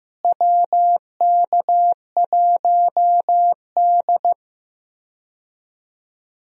Morse code, heard as a keyed tone, about 15 wpm, 700 Hz. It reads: WK1D
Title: USA Call Signs - 4 Characters - Repeated 2x - 15wpm